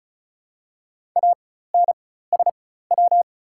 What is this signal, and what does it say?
Morse code, heard as a keyed tone, about 34 wpm, spaced as for 16 wpm, 700 Hz. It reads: ANSW
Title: 4x Character Spacing - Top 300 Words - 35wpm